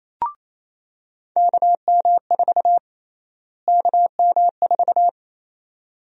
Morse code, heard as a keyed tone, about 28 wpm, 700 Hz. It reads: KM4 KM4
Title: USA Call Sign Prefixes - Repeated 2x - 28wpm